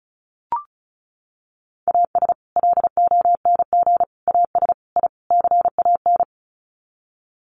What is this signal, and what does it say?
Morse code, heard as a keyed tone, about 34 wpm, 700 Hz. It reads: AS LONG AS I CAN
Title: Sets of 5 Words Encore2 35wpm